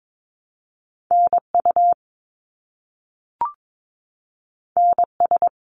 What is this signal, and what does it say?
Morse code, heard as a keyed tone, about 22 wpm, 700 Hz. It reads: NU NS
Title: Canadian Provinces and Territories - Repeated 3x - 22wpm